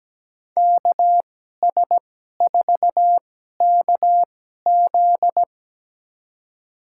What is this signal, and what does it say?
Morse code, heard as a keyed tone, about 17 wpm, spaced as for 12 wpm, 700 Hz. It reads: KS4KZ
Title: USA Call Signs - 5 Characters - 2x Character Spacing - 17wpm